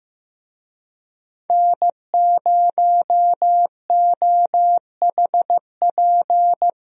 Morse code, heard as a keyed tone, about 15 wpm, 700 Hz. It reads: N0OHP
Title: USA Call Signs - 5 Characters - Repeated 3x - 15wpm